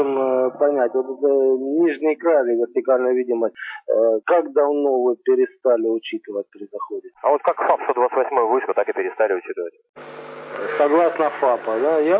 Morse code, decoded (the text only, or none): none